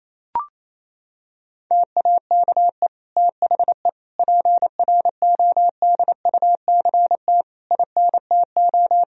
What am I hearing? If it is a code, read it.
Morse code, heard as a keyed tone, about 28 wpm, 700 Hz. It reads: TAKE THE PRODUCT INTO
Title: Sentences from Top 1000 Words - Review Part 2 - 28wpm